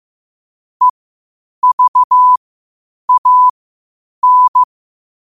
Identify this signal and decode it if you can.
Morse code, heard as a keyed tone, about 15 wpm, spaced as for 8 wpm, 1000 Hz. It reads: EVAN
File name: qotdfiles / 8wpm